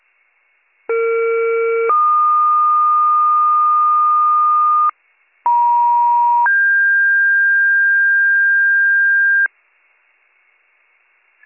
Two-tone page, heard as a one-second tone, then a three-second tone, about 460 then 1220 Hz, about 950 then 1600 Hz. It is heard from 0.9 to 9.5 s.